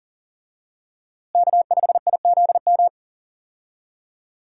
Morse code, heard as a keyed tone, about 39 wpm, 700 Hz. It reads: K5IZM
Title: USA Call Signs - 5 Characters - Repeated 3x - 40wpm